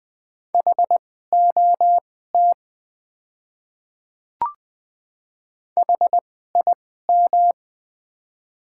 Morse code, heard as a keyed tone, about 20 wpm, spaced as for 14 wpm, 700 Hz. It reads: HOT HIM
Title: 2x Character Spacing - Top 500 Words - 20wpm